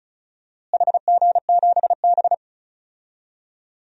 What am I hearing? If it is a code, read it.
Morse code, heard as a keyed tone, about 34 wpm, 700 Hz. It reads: HG7B
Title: DXCC-100 Call Signs - 4 Characters - Repeated 2x - 35wpm